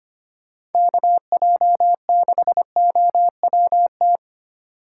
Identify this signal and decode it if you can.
Morse code, heard as a keyed tone, about 32 wpm, 700 Hz. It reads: KJ6OWT